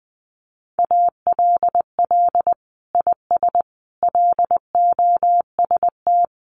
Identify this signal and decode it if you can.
Morse code, heard as a keyed tone, about 20 wpm, 700 Hz. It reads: ALL IS LOST
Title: Sentences from Top 1400 Words 20wpm